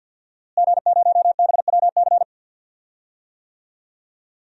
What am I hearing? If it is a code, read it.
Morse code, heard as a keyed tone, about 51 wpm, 700 Hz. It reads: G0BWC
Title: DXCC-100 Call Signs - 5 Characters - Repeated 3x - 50wpm